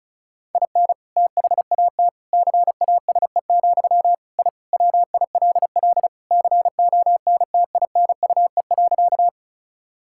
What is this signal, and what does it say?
Morse code, heard as a keyed tone, about 36 wpm, 700 Hz. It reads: IN THAT CASE, I WILL CONTINUE.
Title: Sentences from Top 1000 Words - Review Part 6 - 35wpm